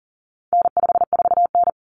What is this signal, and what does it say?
Morse code, heard as a keyed tone, about 40 wpm, 700 Hz. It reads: N54N
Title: Common Cut Numbers - 4-Digit 40wpm